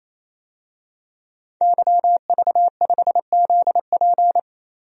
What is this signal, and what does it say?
Morse code, heard as a keyed tone, about 27 wpm, 700 Hz. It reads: YV5ZP
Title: DXCC-100 Call Signs - 5 Characters - 28wpm